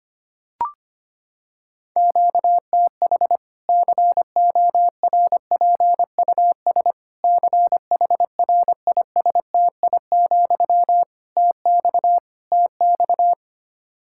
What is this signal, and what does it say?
Morse code, heard as a keyed tone, about 25 wpm, 700 Hz. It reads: QTH CORPUS CHRISTI, TX TX
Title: K1USN SST QSO Practice 25wpm